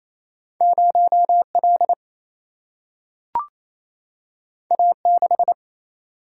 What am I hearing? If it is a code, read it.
Morse code, heard as a keyed tone, about 28 wpm, 700 Hz. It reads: 0 L A 6
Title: Single-Letter Single-Number 28wpm